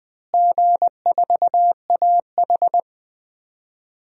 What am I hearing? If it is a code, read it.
Morse code, heard as a keyed tone, about 20 wpm, 700 Hz. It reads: G4AH
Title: DXCC-100 Call Signs - 4 Characters - Repeated 3x - 20wpm